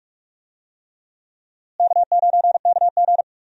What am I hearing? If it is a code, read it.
Morse code, heard as a keyed tone, about 46 wpm, 700 Hz. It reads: K9KG
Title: USA Call Signs - 4 Characters - Repeated 3x - 45wpm